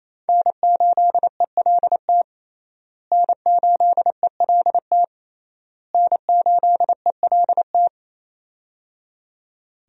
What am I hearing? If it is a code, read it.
Morse code, heard as a keyed tone, about 28 wpm, 700 Hz. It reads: N8ELT N8ELT N8ELT
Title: USA Call Signs - 5 Characters - Repeated 3x - 28wpm